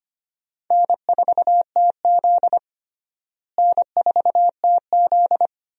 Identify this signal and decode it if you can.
Morse code, heard as a keyed tone, about 25 wpm, 700 Hz. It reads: N4TZ N4TZ
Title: USA Call Signs - 4 Characters - Repeated 2x - 25wpm